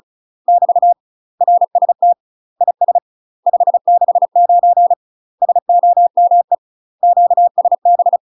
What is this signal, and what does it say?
Morse code, heard as a keyed tone, about 34 wpm, 700 Hz. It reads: <BT> RST IS 569 SOME QSB
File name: UK MASTER QSOs TRAINING / 35WPM / UK MASTER QSOs 0039